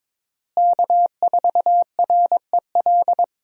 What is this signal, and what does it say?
Morse code, heard as a keyed tone, about 21 wpm, 700 Hz. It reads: K4REL
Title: USA Call Signs - 5 Characters - Repeated 3x - 22wpm